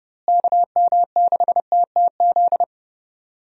Morse code, heard as a keyed tone, about 32 wpm, 700 Hz. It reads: KM6TTZ
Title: USA Call Signs - 6 Characters - Repeated 2x - 30wpm